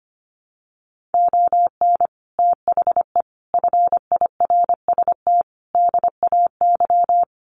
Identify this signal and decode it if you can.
Morse code, heard as a keyed tone, about 25 wpm, 700 Hz. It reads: ON THE FIRST DAY
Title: Sets of 5 Words Encore 25wpm